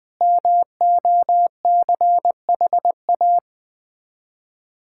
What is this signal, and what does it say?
Morse code, heard as a keyed tone, about 20 wpm, 700 Hz. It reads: MOCHA
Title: Difficult Words with Hints - 20wpm